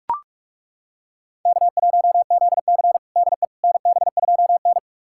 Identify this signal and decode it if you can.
Morse code, heard as a keyed tone, about 41 wpm, 700 Hz. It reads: K1ZC DE NB2N
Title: USA Callsign DE Callsign - 4 Characters - 45wpm